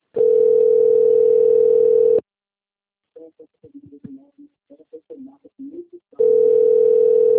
Ringback tone (North America), 440 + 480 Hz.
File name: rg-1600-unknown- - - .2877